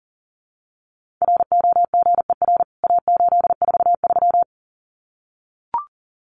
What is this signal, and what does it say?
Morse code, heard as a keyed tone, about 40 wpm, 700 Hz. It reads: ROGER A843
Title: CWT Exchange 40wpm